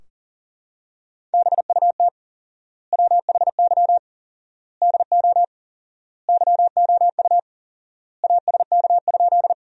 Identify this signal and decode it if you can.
Morse code, heard as a keyed tone, about 40 wpm, 700 Hz. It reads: BUT WHY DO YOU ASK?
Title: 4x Word-Spacing - Sentences from Top 200 Words - 40wpm